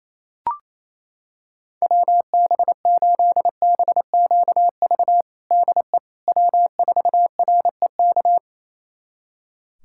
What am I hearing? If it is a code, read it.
Morse code, heard as a keyed tone, about 28 wpm, 700 Hz. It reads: WB8BQV DE W4REK
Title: USA Callsign DE Callsign 28wpm